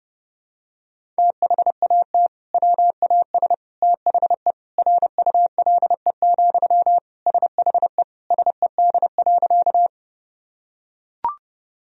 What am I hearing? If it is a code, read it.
Morse code, heard as a keyed tone, about 30 wpm, 700 Hz. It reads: THAT WAS THE RULE, SHE SED.
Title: Sentences from Top 400 Words and Abbreviations 30wpm